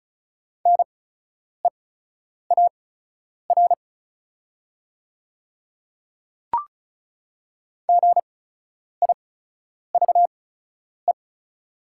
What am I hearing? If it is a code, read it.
Morse code, heard as a keyed tone, about 36 wpm, spaced as for 10 wpm, 690 Hz. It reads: NEAR GIVE
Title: 8x Character Spacing - Top 200 Words - 35wpm